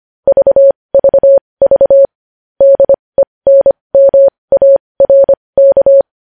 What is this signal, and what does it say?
Morse code, heard as a keyed tone, about 25 wpm, spaced as for 20 wpm, 550 Hz. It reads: VVV DENMARK
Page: news-headlines-in-morse-code-at-20-wpm